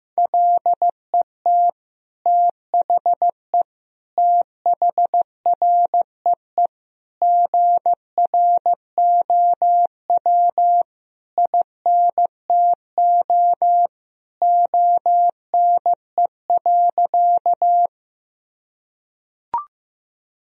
Morse code, heard as a keyed tone, about 16 wpm, 700 Hz. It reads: LET THE THREE GROW INTO ONE.
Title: Sentences from Top 1000 Words - Review Part 1 - 15wpm